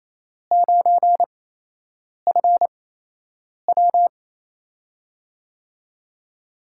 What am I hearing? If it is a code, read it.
Morse code, heard as a keyed tone, about 28 wpm, spaced as for 8 wpm, 700 Hz. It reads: 9FW
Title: USA Call Signs - 5 Characters - 8x Character Spacing - 28wpm